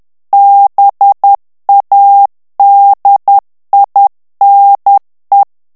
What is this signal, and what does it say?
Morse code, heard as a keyed tone, about 10 wpm, 800 Hz. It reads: BADINE
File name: morse mp3 / FRANCAIS TEXTE CLAIR 2 VITESSE 10 MOTS MINUTES